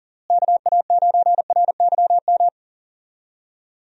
Morse code, heard as a keyed tone, about 40 wpm, 700 Hz. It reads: KA9RYM
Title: USA Call Signs - 6 Characters - Repeated 3x - 40wpm